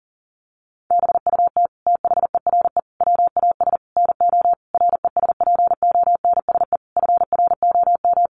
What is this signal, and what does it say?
Morse code, heard as a keyed tone, about 40 wpm, 700 Hz. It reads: BUT THERE WAS NO RESPONSE FROM
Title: Sentences from Top 1400 Words 40wpm